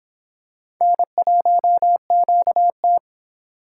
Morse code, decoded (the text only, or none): N1QT